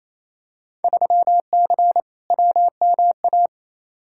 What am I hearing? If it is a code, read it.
Morse code, heard as a keyed tone, about 28 wpm, 700 Hz. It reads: 3C WMA
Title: Field Day Exchanges Encore - 28wpm